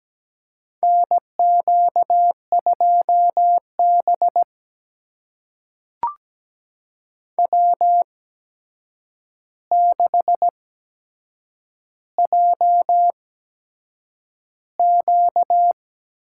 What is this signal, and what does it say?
Morse code, heard as a keyed tone, about 17 wpm, spaced as for 7 wpm, 700 Hz. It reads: NQ2B W6JQ
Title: USA Call Signs - 8x Character Spacing - 17wpm